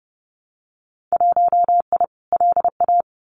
Morse code, heard as a keyed tone, about 30 wpm, 700 Hz. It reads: 1I LA
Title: Winter Field Day Exchanges Encore 30wpm